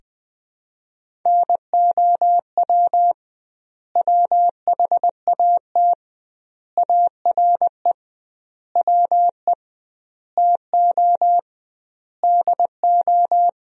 Morse code, heard as a keyed tone, about 21 wpm, 700 Hz. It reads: NOW WHAT ARE WE TO DO
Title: 2x Word-Spacing - Sentences from Top 100 Words - 20wpm